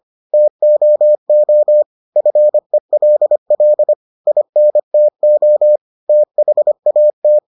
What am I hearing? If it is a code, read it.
Morse code, heard as a keyed tone, about 25 wpm, 600 Hz. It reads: TOO FELL INTO THAT